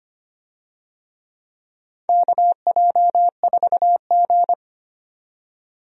Morse code, heard as a keyed tone, about 25 wpm, 700 Hz. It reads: KJ4G
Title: USA Call Signs - 4 Characters - 25wpm